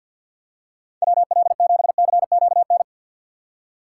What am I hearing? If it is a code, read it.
Morse code, heard as a keyed tone, about 52 wpm, 700 Hz. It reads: WL7CQN
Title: USA Call Signs - 6 Characters - 4x Character Spacing - 50wpm